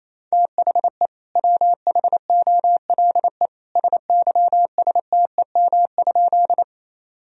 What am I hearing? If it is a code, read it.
Morse code, heard as a keyed tone, about 27 wpm, 700 Hz. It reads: THE WHOLE SYSTEM?
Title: Sentences from Top 1000 Words - Review Part 3 - 28wpm